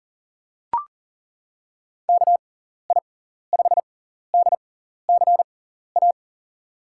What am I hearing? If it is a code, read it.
Morse code, heard as a keyed tone, about 40 wpm, spaced as for 14 wpm, 700 Hz. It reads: KI5DCA
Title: USA Call Signs - 6 Characters - 6x Character Spacing - 40wpm